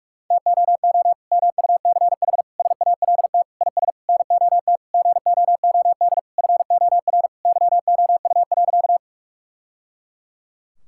Morse code, heard as a keyed tone, about 46 wpm, 700 Hz. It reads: TOO MUCH SALT IS NOT GOOD FOR YOU.
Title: Sentences from Top 1000 Words - Review Part 5 - 45wpm